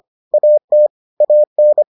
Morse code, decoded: AT AN

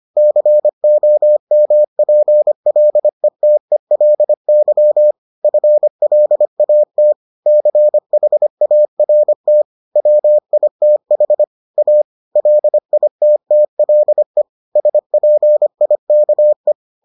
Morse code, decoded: COMPLETELY FLAT CHART WITH A LITTLE SPIKE

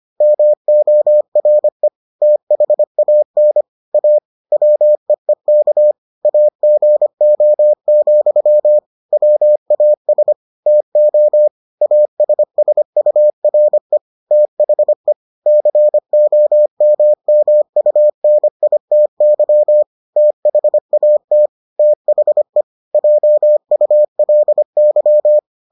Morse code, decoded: MORE THAN A WEEK AGO, WAS TO ASSURE THE COMMUNITY THAT THE JULY

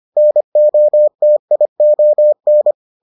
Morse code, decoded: NOTION

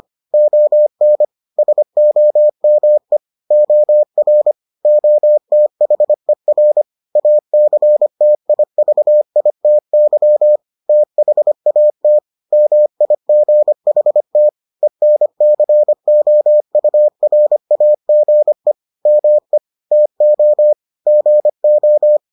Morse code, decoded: ON SOME OR OTHER ACTIVITY THAT MIGHT ENCOURAGE ME TO GO

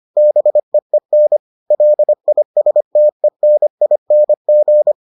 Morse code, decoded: BEEN LISTENING

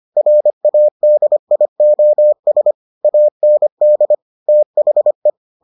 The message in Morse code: RADIOS AND THE

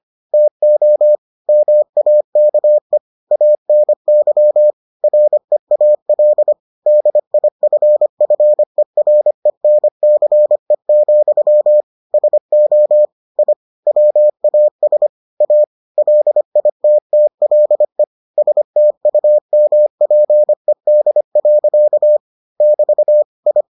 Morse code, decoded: TO MAKE ANY REAL DIFFERENCE, SO I WAS A LITTLE STUMPED. <BT> I

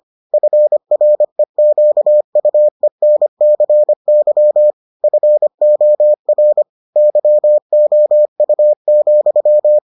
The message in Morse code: FREQUENCY FOR YOU,

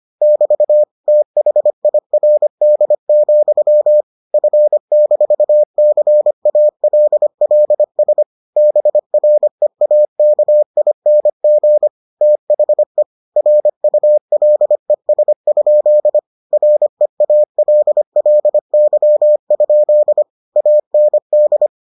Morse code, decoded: <BT> THIRD, F-CALLS BREAKING THE RULES? REALLY? AND

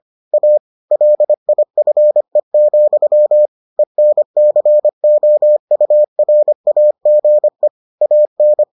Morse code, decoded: A LIFE, ENCOURAGE AN

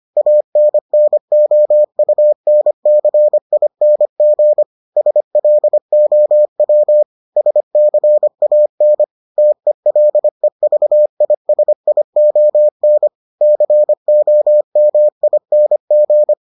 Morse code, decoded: ANNOUNCING SLOW SCAN TELEVISION COMING

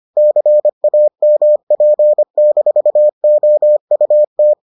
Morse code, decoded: CAMP-OUT

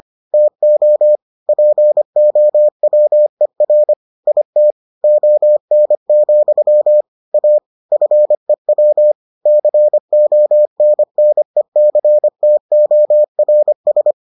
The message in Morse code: TO POWER IT ON, A FEW CONNECTORS

